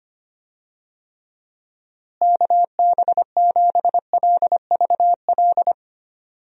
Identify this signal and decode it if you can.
Morse code, heard as a keyed tone, about 25 wpm, 700 Hz. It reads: KB7LVL
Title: USA Call Signs - 6 Characters - Repeated 3x - 25wpm